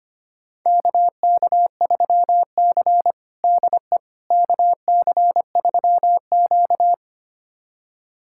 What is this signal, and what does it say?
Morse code, heard as a keyed tone, about 25 wpm, 700 Hz. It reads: KK3C DE KC3Q
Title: USA Callsign DE Callsign - 4 Characters - 25wpm